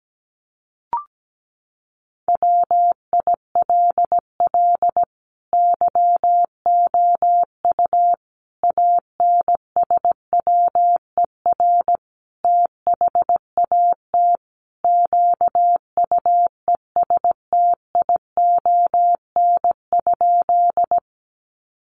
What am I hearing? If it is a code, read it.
Morse code, heard as a keyed tone, about 17 wpm, 700 Hz. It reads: WILL YOU ANSWER THAT QUESTION?
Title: Sentences from Top 400 Words 17wpm